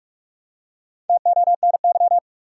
Morse code, heard as a keyed tone, about 41 wpm, 700 Hz. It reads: TONY
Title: USA Common Names - Max 4 Letters - 45wpm